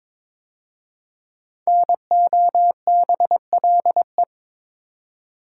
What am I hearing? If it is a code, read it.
Morse code, heard as a keyed tone, about 22 wpm, 700 Hz. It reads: NOBLE
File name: 5 Letter Words - First 3 Letters Given - 22wpm